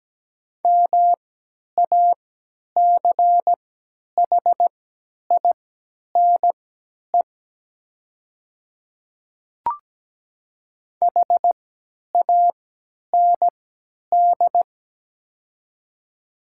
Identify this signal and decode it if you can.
Morse code, heard as a keyed tone, about 17 wpm, spaced as for 10 wpm, 700 Hz. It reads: MACHINE HAND
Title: 3x Character Spacing - Top 500 Words - 17wpm